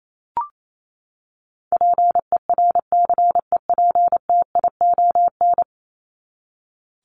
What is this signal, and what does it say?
Morse code, heard as a keyed tone, about 28 wpm, 700 Hz. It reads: PERCEPTION
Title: Top 1751-2000 Words 28wpm